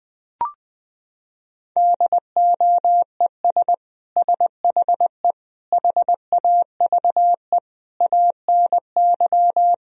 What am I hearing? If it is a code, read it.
Morse code, heard as a keyed tone, about 20 wpm, 700 Hz. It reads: DOES SHE HAVE ANY
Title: Sentences from Top 1000 Words - Review Part 3 - 20wpm